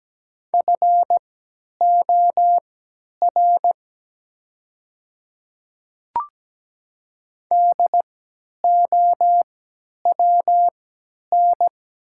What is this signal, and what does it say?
Morse code, heard as a keyed tone, about 17 wpm, spaced as for 10 wpm, 700 Hz. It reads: FOR DOWN
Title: 3x Character Spacing - Top 100 Words - 17wpm